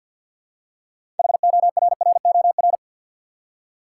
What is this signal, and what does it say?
Morse code, heard as a keyed tone, about 51 wpm, 700 Hz. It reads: HORROR
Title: Six Letter Words - 50wpm